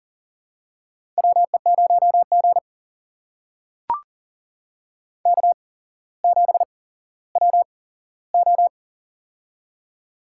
Morse code, heard as a keyed tone, about 40 wpm, spaced as for 17 wpm, 700 Hz. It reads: WE0G K7WO
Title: USA Call Signs - 4 Characters - 8x Character Spacing - 40wpm